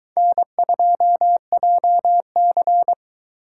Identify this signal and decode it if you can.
Morse code, heard as a keyed tone, about 22 wpm, 700 Hz. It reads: N2JC